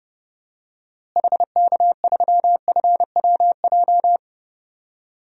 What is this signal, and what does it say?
Morse code, heard as a keyed tone, about 32 wpm, 700 Hz. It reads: HK3FWJ
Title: DXCC-100 Call Signs - 6 Characters - Repeated 2x - 30wpm